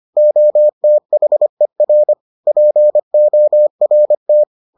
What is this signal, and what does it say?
Morse code, heard as a keyed tone, about 25 wpm, 600 Hz. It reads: OTHER PORT